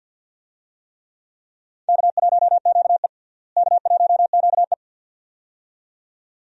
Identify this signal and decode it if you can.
Morse code, heard as a keyed tone, about 46 wpm, 700 Hz. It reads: K1QE K1QE
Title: USA Call Signs - 4 Characters - Repeated 2x - 50wpm